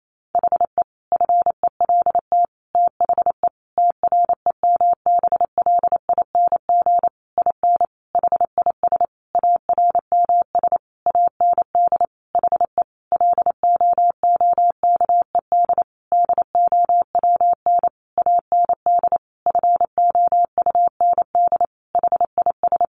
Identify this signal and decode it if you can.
Morse code, heard as a keyed tone, about 28 wpm, 700 Hz. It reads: HE FELT THE TREMBLING IN HIS ARMS AND HE LOOKED DOWN AND FOUND HIS
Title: A Sound Of Thunder 28wpm